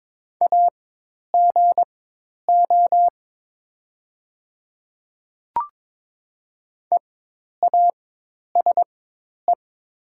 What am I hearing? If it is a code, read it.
Morse code, heard as a keyed tone, about 22 wpm, spaced as for 10 wpm, 700 Hz. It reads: AGO EASE
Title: 4x Character Spacing - Top 500 Words - 22wpm